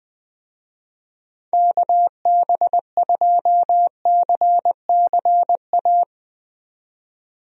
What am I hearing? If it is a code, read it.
Morse code, heard as a keyed tone, about 20 wpm, 700 Hz. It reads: KB2CCA
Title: USA Call Signs - 6 Characters - 20wpm